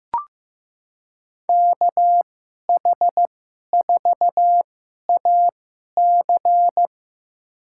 Morse code, heard as a keyed tone, about 15 wpm, spaced as for 11 wpm, 700 Hz. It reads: KH4AC
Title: USA Call Signs - 5 Characters - 2x Character Spacing - 15wpm